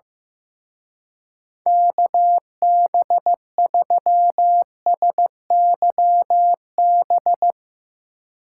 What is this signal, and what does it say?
Morse code, heard as a keyed tone, about 15 wpm, 700 Hz. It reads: KB3SYB
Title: Speed Racing - USA Call Signs 15wpm